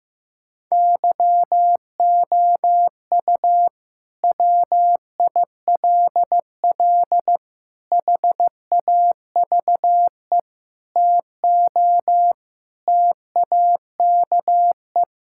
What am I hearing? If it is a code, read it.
Morse code, heard as a keyed tone, about 15 wpm, 700 Hz. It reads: YOU WILL HAVE TO TAKE
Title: Sentences from Top 1000 Words - Review Part 5 - 15wpm